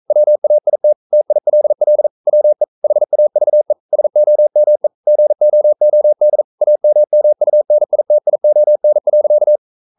Morse code, heard as a keyed tone, about 42 wpm, 600 Hz. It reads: WAIT TILL WE HAVE SOME GOOD AMMUNITION.
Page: Romanen Never af Ken Follet på engelsk.